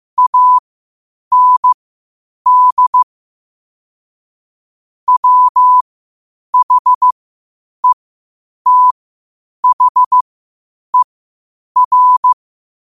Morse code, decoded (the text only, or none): AND WHETHER